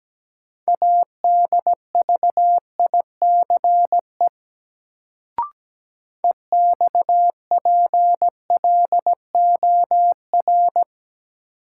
Morse code, heard as a keyed tone, about 17 wpm, 700 Hz. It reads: ADVICE EXPLOR
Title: Sentences from Top 1100 Words - New Word Drill - 17wpm